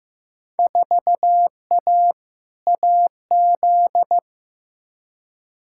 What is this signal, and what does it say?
Morse code, heard as a keyed tone, about 15 wpm, 700 Hz. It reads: 4A AZ
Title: Field Day Exchanges Encore - 15wpm